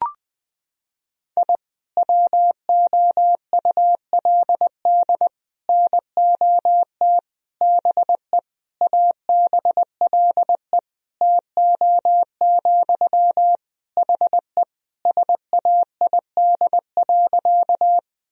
Morse code, heard as a keyed tone, about 21 wpm, 700 Hz. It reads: I WOULD NOT BE ABLE TO, HE SAID.
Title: Sentences from Top 1000 Words - Review Part 2 - 20wpm